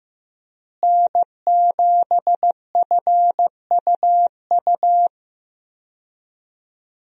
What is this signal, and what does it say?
Morse code, heard as a keyed tone, about 15 wpm, 700 Hz. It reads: N7FUU
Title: USA Call Signs - 5 Characters - Repeated 3x - 15wpm